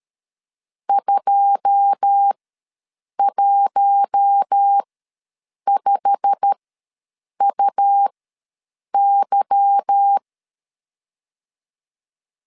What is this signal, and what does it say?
Morse code, heard as a keyed tone, about 13 wpm, spaced as for 7 wpm, 800 Hz. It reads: 215UY
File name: LEZIONI / 9 2 ws 12wpm 3406